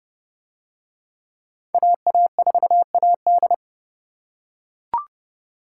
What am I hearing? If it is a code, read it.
Morse code, heard as a keyed tone, about 27 wpm, 700 Hz. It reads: AA4AD